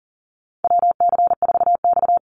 Morse code, heard as a keyed tone, about 40 wpm, 700 Hz. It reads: WC4X
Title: Call Signs Repeated 2x - 40wpm